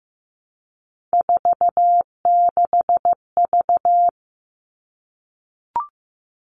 Morse code, heard as a keyed tone, about 15 wpm, 700 Hz. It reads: 46V
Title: All Cut Numbers - 3-Digit 15wpm